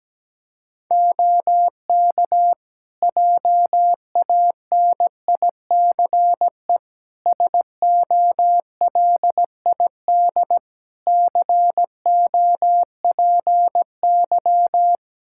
Morse code, heard as a keyed tone, about 17 wpm, 700 Hz. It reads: OK JANICE SOLID COPY
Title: Practice QSO - Exchange 3 - More Info - 17wpm